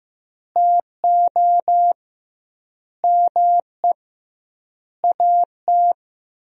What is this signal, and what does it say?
Morse code, heard as a keyed tone, about 16 wpm, 700 Hz. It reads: TO ME AT
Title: 2x Word-Spacing - Sentences from Top 500 Words - 15wpm